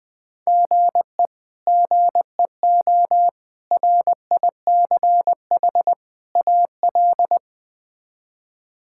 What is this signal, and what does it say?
Morse code, heard as a keyed tone, about 20 wpm, 700 Hz. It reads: GE GEO RICH AL
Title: K1USN SST QSO Practice - 20wpm